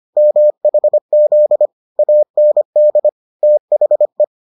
Morse code, heard as a keyed tone, about 25 wpm, 600 Hz. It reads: MHZ AND THE